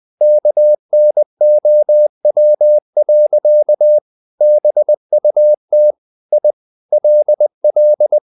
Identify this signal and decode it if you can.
Morse code, heard as a keyed tone, about 20 wpm, 600 Hz. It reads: KNOW. BUT I LL
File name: the old man and the sea / the old man and the sea part 1 20wpm